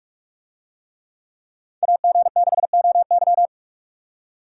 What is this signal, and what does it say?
Morse code, heard as a keyed tone, about 45 wpm, 700 Hz. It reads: AG6OY